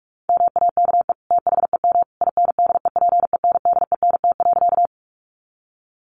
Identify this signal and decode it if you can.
Morse code, heard as a keyed tone, about 44 wpm, 700 Hz. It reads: MAKE THEM INDEPENDENT.
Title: Sentences from Top 900 Words 45wpm